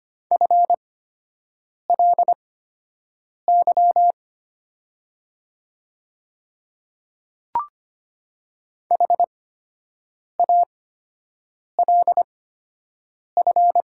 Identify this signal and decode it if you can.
Morse code, heard as a keyed tone, about 25 wpm, spaced as for 7 wpm, 700 Hz. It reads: FLY HALF
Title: 8x Character Spacing - Top 500 Words - 25wpm